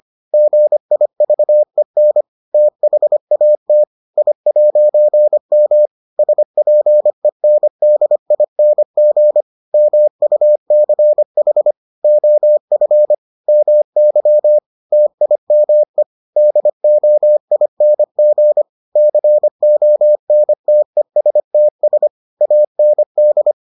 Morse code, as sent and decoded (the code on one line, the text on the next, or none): --. .. ...- . -. / - .... .- - / .. .----. -- / ... .--. -. -.. .. -. --. / -- ..- -.-. .... / --- ..-. / -- -.-- / - .. -- . / -.. --- .. -. --. / -.-. --- -. - . ... - ... / .- -. -..
GIVEN THAT I'M SPENDING MUCH OF MY TIME DOING CONTESTS AND